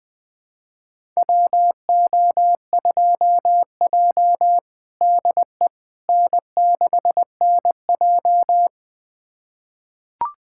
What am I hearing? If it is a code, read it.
Morse code, heard as a keyed tone, about 21 wpm, 700 Hz. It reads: WO2J DE N6NJ
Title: USA Callsign DE Callsign - 4 Characters - 20wpm